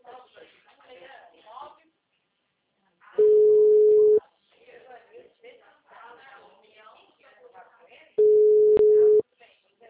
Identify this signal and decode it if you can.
Ringback tone (Europe, 425 Hz), with speech.
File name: out- -2030- - - .18782